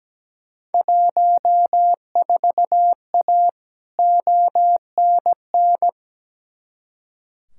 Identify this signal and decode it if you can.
Morse code, heard as a keyed tone, about 17 wpm, 700 Hz. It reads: 14A ONN
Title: Field Day Exchanges Encore - 17wpm